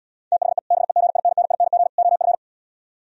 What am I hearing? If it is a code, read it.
Morse code, heard as a keyed tone, about 76 wpm, 700 Hz. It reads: THE BEGINNING OF